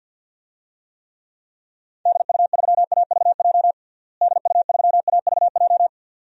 Morse code, heard as a keyed tone, about 50 wpm, 700 Hz. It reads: DU3AVJ DU3AVJ
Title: DXCC-100 Call Signs - 6 Characters - Repeated 2x - 50wpm